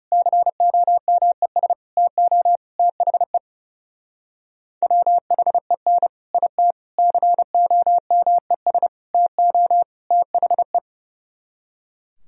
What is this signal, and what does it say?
Morse code, heard as a keyed tone, about 38 wpm, 700 Hz. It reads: COMES TO THE WHEN IT COMES TO THE
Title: Speed-Racing - Sets of 5 Words 30wpm